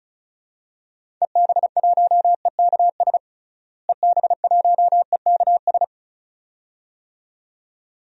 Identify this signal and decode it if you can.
Morse code, heard as a keyed tone, about 36 wpm, 700 Hz. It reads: EB1EKS EB1EKS
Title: DXCC-100 Call Signs - 6 Characters - Repeated 2x - 35wpm